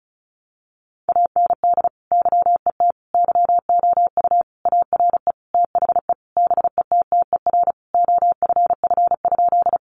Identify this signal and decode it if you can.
Morse code, heard as a keyed tone, about 34 wpm, 700 Hz. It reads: AND YET YOU ARE THE BETTER OFF?
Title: Sentences from Top 500 Words 35wpm